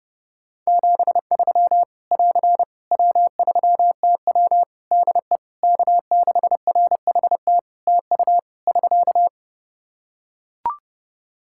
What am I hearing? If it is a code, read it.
Morse code, heard as a keyed tone, about 31 wpm, 700 Hz. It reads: 73 <AR> W3TW DE K6RHT TU <SK>
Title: Practice QSO - Exchange 4 - Ending - 30wpm